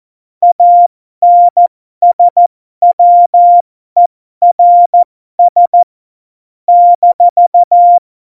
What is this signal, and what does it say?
Morse code, heard as a keyed tone, about 14 wpm, spaced as for 12 wpm, 700 Hz. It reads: ANSWERS -
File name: Morsekurs / Witze